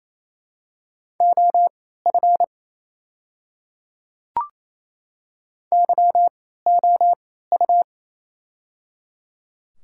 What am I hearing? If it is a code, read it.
Morse code, heard as a keyed tone, about 28 wpm, spaced as for 16 wpm, 700 Hz. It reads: OF YOU
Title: 3x Character Spacing - Top 100 Words - 28wpm